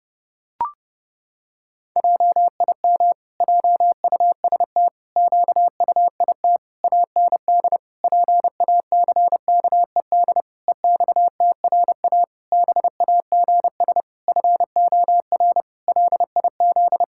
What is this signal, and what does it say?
Morse code, heard as a keyed tone, about 28 wpm, 700 Hz. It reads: JIM JUST QUIT AND PACKED EXTRA BAGS FOR LIZ
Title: Pangram Challenge v2 - 30wpm